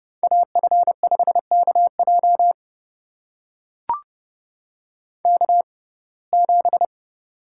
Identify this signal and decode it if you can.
Morse code, heard as a keyed tone, about 30 wpm, spaced as for 22 wpm, 700 Hz. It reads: AF5KJ K7
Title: USA Call Signs - 5 Characters - 6x Character Spacing - 30wpm